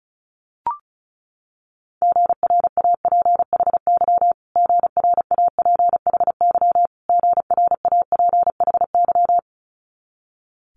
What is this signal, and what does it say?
Morse code, heard as a keyed tone, about 35 wpm, 700 Hz. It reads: GRAPHY GRAPHY GRAPHY
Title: Multi-letter Phonemes Repeated 3x 35wpm